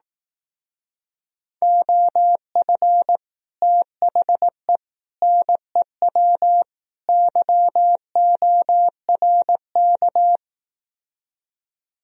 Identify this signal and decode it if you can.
Morse code, heard as a keyed tone, about 18 wpm, 700 Hz. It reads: OF THE NEW YORK
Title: Speed-Racing - Sets of 4 Words 15wpm